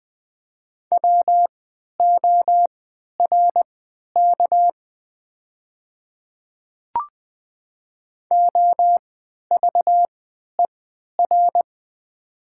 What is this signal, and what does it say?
Morse code, heard as a keyed tone, about 20 wpm, spaced as for 11 wpm, 700 Hz. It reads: WORK OVER
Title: 3x Character Spacing - Top 200 Words - 20wpm